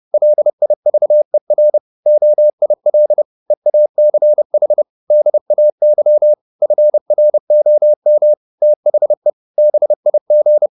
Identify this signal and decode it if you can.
Morse code, heard as a keyed tone, about 30 wpm, 600 Hz. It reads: LIVER OIL EACH DAY FROM THE BIG
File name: the old man and the sea / the old man and the sea 1 30wpm